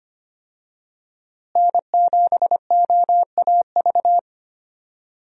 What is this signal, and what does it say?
Morse code, heard as a keyed tone, about 25 wpm, 700 Hz. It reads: N7OAV
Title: USA Call Signs - 5 Characters - Repeated 2x - 25wpm